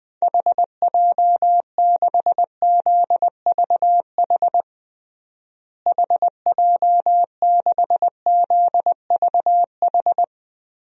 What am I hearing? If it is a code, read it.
Morse code, heard as a keyed tone, about 20 wpm, 700 Hz. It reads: HJ6ZVH HJ6ZVH
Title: DXCC-100 Call Signs - 6 Characters - Repeated 3x - 20wpm